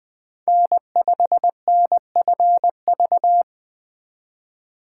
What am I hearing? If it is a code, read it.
Morse code, heard as a keyed tone, about 20 wpm, 700 Hz. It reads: N5NFV
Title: USA Call Signs - 5 Characters - 3x Character Spacing - 20wpm